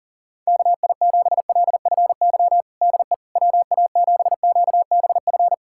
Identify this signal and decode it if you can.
Morse code, heard as a keyed tone, about 37 wpm, 700 Hz. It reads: KI7LFY DE WA7QBF
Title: USA Callsign DE Callsign 40wpm